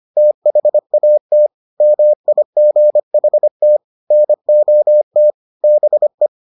Morse code, heard as a keyed tone, about 25 wpm, 600 Hz. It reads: THAT MIGHT NOT BE